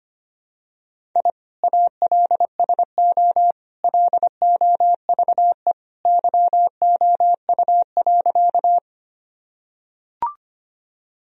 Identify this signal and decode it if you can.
Morse code, heard as a keyed tone, about 25 wpm, 700 Hz. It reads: I ALSO LOVE YOU.
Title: Sentences from Top 1000 Words - Review Part 1 - 25wpm